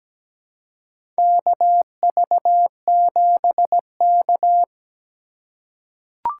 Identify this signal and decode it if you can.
Morse code, heard as a keyed tone, about 17 wpm, 700 Hz. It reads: KV7K
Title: USA Call Signs - 4 Characters - 3x Character Spacing - 17wpm